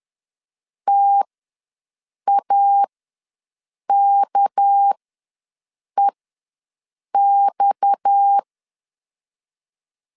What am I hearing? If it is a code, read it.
Morse code, heard as a keyed tone, about 10 wpm, spaced as for 6 wpm, 800 Hz. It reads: TAKEX
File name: LEZIONI / 6 PKX0 / 6 PKX ws 10wpm 3706